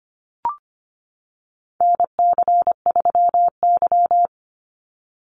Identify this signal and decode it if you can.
Morse code, heard as a keyed tone, about 25 wpm, 700 Hz. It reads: NC3Y
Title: Call Signs - 25wpm